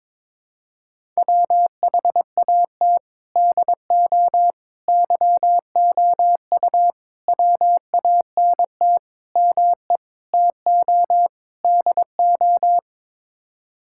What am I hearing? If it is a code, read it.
Morse code, heard as a keyed tone, about 21 wpm, 700 Hz. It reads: WHAT DO YOU WANT ME TO DO
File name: practice / 22wpm / Sets of 7 Words - 22wpm